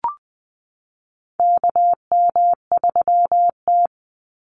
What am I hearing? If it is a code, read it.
Morse code, heard as a keyed tone, about 20 wpm, 700 Hz. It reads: KM3T
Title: Call Signs Repeated 2x - 20wpm